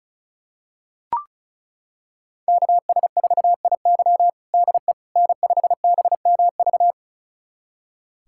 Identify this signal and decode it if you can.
Morse code, heard as a keyed tone, about 36 wpm, 700 Hz. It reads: KS4IY DE N5BMV
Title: USA Callsign DE Callsign - 5 Characters - 35wpm